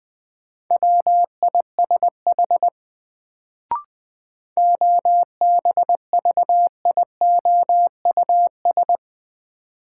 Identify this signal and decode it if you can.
Morse code, heard as a keyed tone, about 20 wpm, 700 Hz. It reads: WISH OBVIOUS
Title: Sentences from Top 1600 Words - New Word Drill - 20wpm